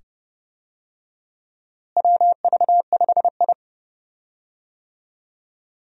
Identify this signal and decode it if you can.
Morse code, heard as a keyed tone, about 30 wpm, 700 Hz. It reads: WV5I
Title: USA Call Signs - 4 Characters - Repeated 2x - 30wpm